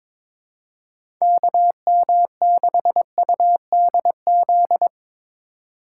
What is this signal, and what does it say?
Morse code, heard as a keyed tone, about 21 wpm, 700 Hz. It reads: KM6UDZ
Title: USA Call Signs - 3x Character Spacing - 22wpm